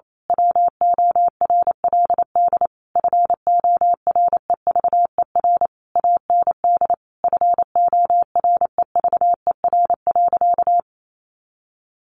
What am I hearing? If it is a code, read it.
Morse code, heard as a keyed tone, about 28 wpm, 700 Hz. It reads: WORLD FOREVER AND FOREVER.
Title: A Sound Of Thunder 28wpm